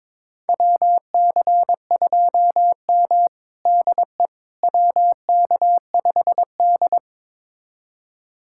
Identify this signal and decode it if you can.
Morse code, heard as a keyed tone, about 22 wpm, 700 Hz. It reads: WC2M DE WK5D
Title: USA Callsign DE Callsign - 4 Characters - 22wpm